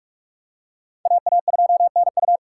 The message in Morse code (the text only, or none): AA2NU